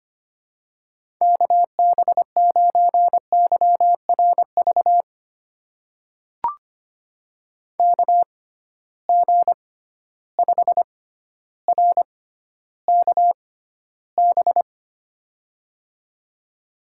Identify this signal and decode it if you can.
Morse code, heard as a keyed tone, about 25 wpm, spaced as for 13 wpm, 700 Hz. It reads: KB9YRV KG5RKB
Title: USA Call Signs - 6 Characters - 6x Character Spacing - 25wpm